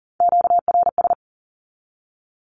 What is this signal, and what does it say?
Morse code, heard as a keyed tone, about 40 wpm, 700 Hz. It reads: QRS
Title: Q-Codes 40wpm